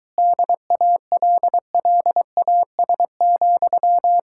Morse code, heard as a keyed tone, about 23 wpm, 700 Hz. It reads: DALLAS,